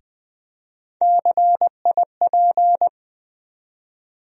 Morse code, heard as a keyed tone, about 20 wpm, 700 Hz. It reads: CIP
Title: USA Call Sign Postfixes - 3 Letters 20wpm